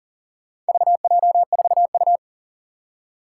Morse code, heard as a keyed tone, about 40 wpm, 700 Hz. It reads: VJ4U
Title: Four Challenging Letter-Number Combinations 40wpm